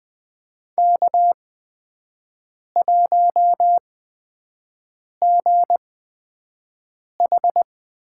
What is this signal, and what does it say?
Morse code, heard as a keyed tone, about 20 wpm, spaced as for 5 wpm, 700 Hz. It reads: K1GH
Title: USA Call Signs - 5 Characters - 8x Character Spacing - 20wpm